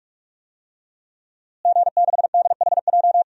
Morse code, heard as a keyed tone, about 43 wpm, 700 Hz. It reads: G6DHJ
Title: DXCC-100 Call Signs - 5 Characters - Repeated 3x - 45wpm